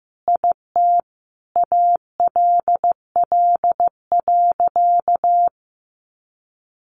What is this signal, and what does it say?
Morse code, heard as a keyed tone, about 15 wpm, 700 Hz. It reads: IT ALL.
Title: Sentences from Top 300 Words 15wpm